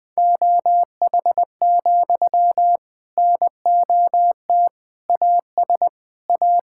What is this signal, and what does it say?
Morse code, heard as a keyed tone, about 20 wpm, 700 Hz. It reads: OH, NOT AS A
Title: Sentences from Top 1000 Words - Review Part 5 - 20wpm